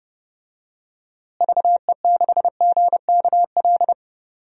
Morse code, heard as a keyed tone, about 30 wpm, 700 Hz. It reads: VE6GKL
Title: DXCC-100 Call Signs - 6 Characters - Repeated 2x - 30wpm